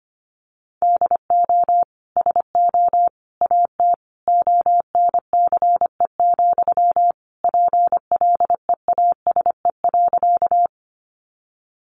Morse code, heard as a keyed tone, about 25 wpm, 700 Hz. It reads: DO SO AT ONCE, PLEASE.
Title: Sentences from Top 1500 Words 25wpm